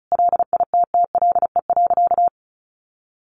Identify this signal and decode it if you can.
Morse code, heard as a keyed tone, about 34 wpm, 700 Hz. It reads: LITTLE.
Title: Sentences from Top 1500 Words 35wpm